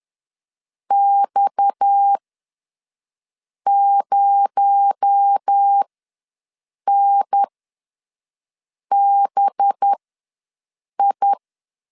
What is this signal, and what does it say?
Morse code, heard as a keyed tone, about 10 wpm, spaced as for 5 wpm, 800 Hz. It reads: X0NBI
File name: LEZIONI / 7 GF9Z / 7 G ws 10wpm 3843